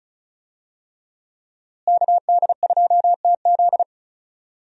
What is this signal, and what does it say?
Morse code, heard as a keyed tone, about 34 wpm, 700 Hz. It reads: KD2TZ
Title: USA Call Signs - 5 Characters - Repeated 3x - 35wpm